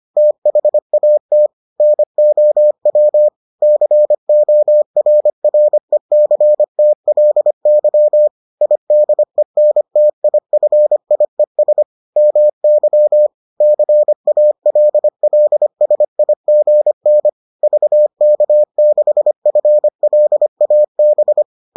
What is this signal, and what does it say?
Morse code, heard as a keyed tone, about 25 wpm, 600 Hz. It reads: THAT NOW CORRECTLY IDENTIFIES MY CALLSIGN VK6FLAB